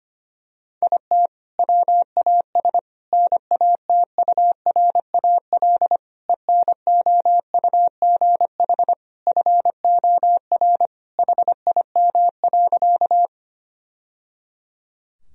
Morse code, heard as a keyed tone, about 25 wpm, 700 Hz. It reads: IT WAS NATURAL ENOUGH FOR HIM.
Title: Sentences from Top 1000 Words - Review Part 4 - 25wpm